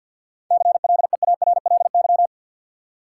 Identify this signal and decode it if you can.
Morse code, heard as a keyed tone, about 52 wpm, 700 Hz. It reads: CLEARLY
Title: Sentences from Top 1300 Words - New Word Drill - 50wpm